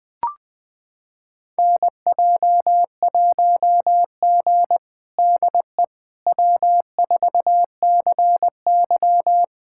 Morse code, heard as a keyed tone, about 20 wpm, 700 Hz. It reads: NJ1G DE W4CY
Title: USA Callsign DE Callsign - 4 Characters - 20wpm